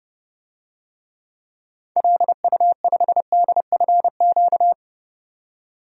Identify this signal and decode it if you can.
Morse code, heard as a keyed tone, about 30 wpm, 700 Hz. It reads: LU5DFQ
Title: DXCC-100 Call Signs - 6 Characters - 30wpm